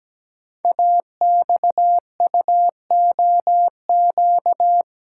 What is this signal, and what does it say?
Morse code, heard as a keyed tone, about 17 wpm, 700 Hz. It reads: AXUOQ
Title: Code Groups - 5 Letters - 17wpm